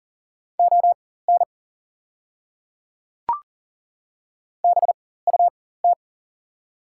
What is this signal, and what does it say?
Morse code, heard as a keyed tone, about 40 wpm, spaced as for 19 wpm, 690 Hz. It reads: ON BUT
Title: 4x Character Spacing - Top 100 Words - 40wpm